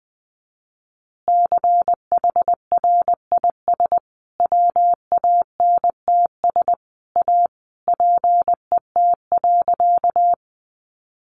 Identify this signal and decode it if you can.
Morse code, heard as a keyed tone, about 20 wpm, 700 Hz. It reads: CHRIS WANTS A PET.
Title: Easy Phrases 20wpm